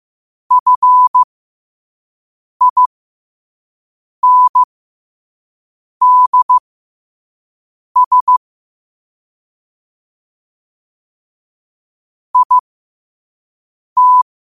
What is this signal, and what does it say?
Morse code, heard as a keyed tone, about 15 wpm, spaced as for 5 wpm, 1000 Hz. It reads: FINDS IT
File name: qotdfiles / 5wpm